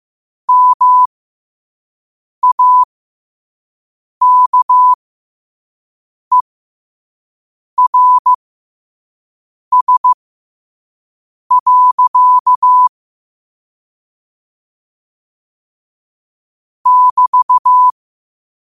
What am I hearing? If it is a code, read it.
Morse code, heard as a keyed tone, about 15 wpm, spaced as for 5 wpm, 1000 Hz. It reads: MAKERS. <BT>
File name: qotdfiles / 5wpm